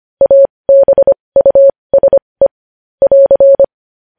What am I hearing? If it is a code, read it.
Morse code, heard as a keyed tone, about 25 wpm, spaced as for 20 wpm, 550 Hz. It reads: ABUSE <AR>
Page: News headlines in Morse code at 20 wpm